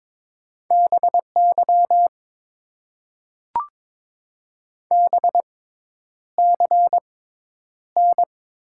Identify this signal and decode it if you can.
Morse code, heard as a keyed tone, about 22 wpm, spaced as for 10 wpm, 700 Hz. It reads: BY BCN
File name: Top 90 QSO Elements - 6x Character Spacing - 22wpm